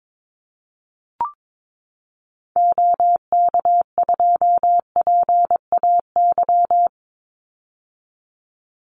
Morse code, heard as a keyed tone, about 22 wpm, 700 Hz. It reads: OK2PAY
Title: Call Signs - 22wpm